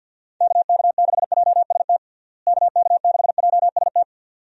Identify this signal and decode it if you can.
Morse code, heard as a keyed tone, about 47 wpm, 700 Hz. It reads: KK6JST KK6JST
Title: USA Call Signs - 6 Characters - Repeated 2x - 50wpm